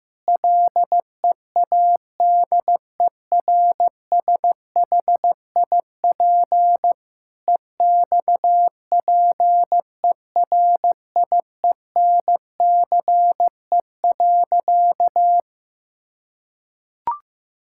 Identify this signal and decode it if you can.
Morse code, heard as a keyed tone, about 15 wpm, 700 Hz. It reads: LEADERSHIP EXPERIENCE.
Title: Sentences from Top 1600 Words 15wpm